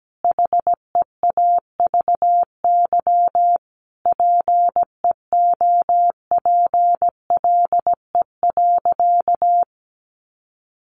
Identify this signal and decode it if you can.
Morse code, heard as a keyed tone, about 17 wpm, 700 Hz. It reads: HEAVY PEOPLE.
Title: Sentences from Top 1100 Words 17wpm